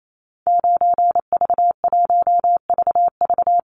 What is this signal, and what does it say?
Morse code, heard as a keyed tone, about 28 wpm, 700 Hz. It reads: 9V1VV
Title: Call Signs Repeated 2x - 28wpm